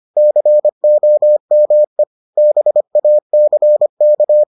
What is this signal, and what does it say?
Morse code, heard as a keyed tone, about 25 wpm, 600 Hz. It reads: COME BACK